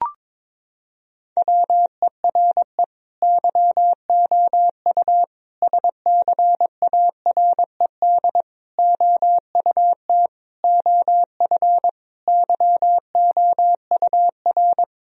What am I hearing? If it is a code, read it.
Morse code, heard as a keyed tone, about 22 wpm, 700 Hz. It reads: WERE YOU SCARED OUT OF YOUR
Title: Sentences from Top 1000 Words - Review Part 4 - 22wpm